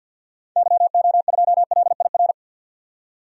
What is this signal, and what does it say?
Morse code, heard as a keyed tone, about 45 wpm, 700 Hz. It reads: YO2LIR
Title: DXCC-100 Call Signs - 6 Characters - 50wpm